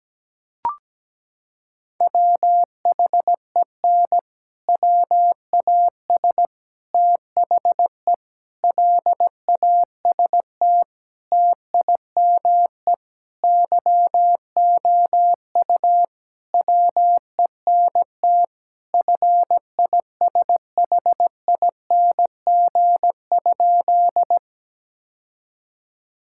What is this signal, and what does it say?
Morse code, heard as a keyed tone, about 17 wpm, 700 Hz. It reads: WHEN WAS THE LAST TIME YOU WENT FISHING?
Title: Sentences from Top 1000 Words - Review Part 5 - 17wpm